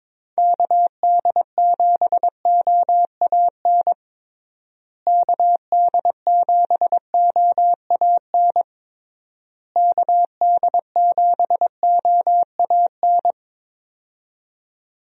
Morse code, heard as a keyed tone, about 22 wpm, 700 Hz. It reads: KD7OAN KD7OAN KD7OAN
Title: USA Call Signs - 6 Characters - Repeated 3x - 22wpm